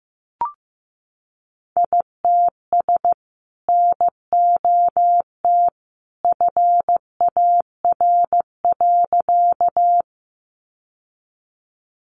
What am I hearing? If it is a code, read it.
Morse code, heard as a keyed tone, about 15 wpm, 700 Hz. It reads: ITS NOT FAR.
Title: Sentences from Top 600 Words 15wpm